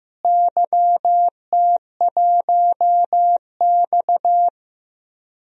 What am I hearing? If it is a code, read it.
Morse code, heard as a keyed tone, about 15 wpm, 700 Hz. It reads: YT1X
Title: DXCC-100 Call Signs - 4 Characters - Repeated 2x - 15wpm